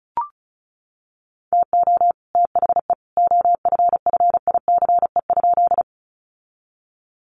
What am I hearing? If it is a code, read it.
Morse code, heard as a keyed tone, about 34 wpm, 700 Hz. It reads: TO THE OFFICE?
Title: Sentences from Top 1000 Words 35wpm